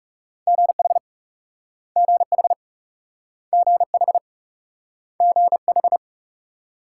Morse code, heard as a keyed tone, about 38 wpm, 700 Hz. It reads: GH GH GH GH